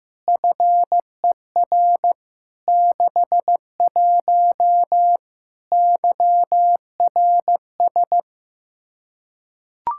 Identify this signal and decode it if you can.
Morse code, heard as a keyed tone, about 15 wpm, 700 Hz. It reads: FER 61 YRS
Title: Practice QSO - Exchange 3 - More Info - 15wpm